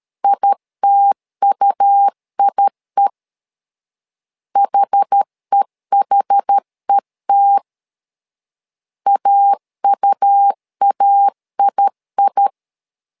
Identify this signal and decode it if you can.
Morse code, heard as a keyed tone, about 13 wpm, 800 Hz. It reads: ITUIE HEHET AUAII
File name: LEZIONI / 2 UHDN / 2 UH 12wpm 1439